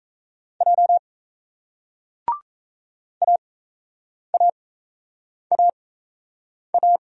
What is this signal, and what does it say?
Morse code, heard as a keyed tone, about 37 wpm, spaced as for 8 wpm, 700 Hz. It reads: J AAAA